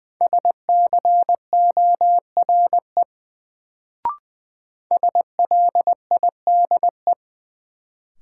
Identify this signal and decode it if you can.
Morse code, heard as a keyed tone, about 20 wpm, 700 Hz. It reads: SCORE SLIDE
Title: Sentences from Top 1600 Words - New Word Drill - 20wpm